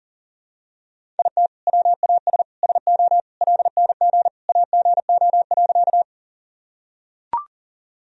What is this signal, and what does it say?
Morse code, heard as a keyed tone, about 40 wpm, 700 Hz. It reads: IT WAS SO LNG AGO.
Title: Sentences from Top 500 Words and Abbreviations 40wpm